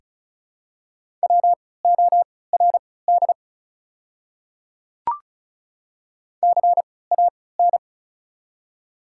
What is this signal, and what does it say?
Morse code, heard as a keyed tone, about 36 wpm, spaced as for 20 wpm, 700 Hz. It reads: WORD CAN
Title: 3x Character Spacing - Top 100 Words - 35wpm